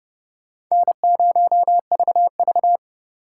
Morse code, heard as a keyed tone, about 29 wpm, 700 Hz. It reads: N0VV